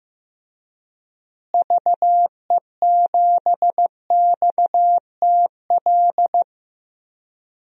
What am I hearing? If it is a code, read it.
Morse code, heard as a keyed tone, about 15 wpm, 700 Hz. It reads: VE7XTL
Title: DXCC-100 Call Signs - 6 Characters - Repeated 3x - 15wpm